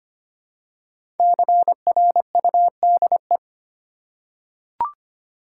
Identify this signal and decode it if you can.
Morse code, heard as a keyed tone, about 25 wpm, 700 Hz. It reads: CRUDE
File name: 5 Letter Words - First 3 Letters Given - 25wpm